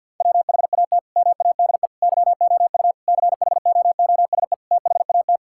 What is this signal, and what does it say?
Morse code, heard as a keyed tone, about 50 wpm, 700 Hz. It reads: WHAT MADE YOU CHOOSE THAT
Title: Sentences from Top 1000 Words - Review Part 5 - 50wpm